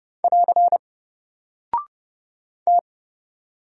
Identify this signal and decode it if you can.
Morse code, heard as a keyed tone, about 30 wpm, 700 Hz. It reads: <AR> T
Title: Single Character (Letter:Number:Punctuation:Prosign) 30wpm